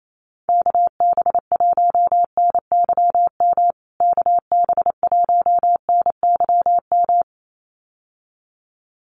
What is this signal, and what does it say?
Morse code, heard as a keyed tone, about 28 wpm, 700 Hz. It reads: KB1NYM KB1NYM
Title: USA Call Signs Repeated 2x 28wpm